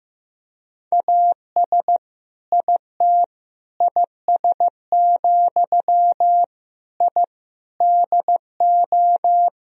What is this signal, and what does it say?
Morse code, heard as a keyed tone, about 15 wpm, 700 Hz. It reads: AS IT IS, I DO
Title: Sentences from Top 1000 Words - Review Part 2 - 15wpm